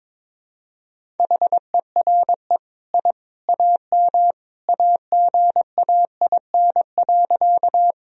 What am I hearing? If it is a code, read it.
Morse code, heard as a keyed tone, about 22 wpm, 700 Hz. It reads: HERE I AM AGAIN.
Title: Sentences from Top 1000 Words - Review Part 2 - 22wpm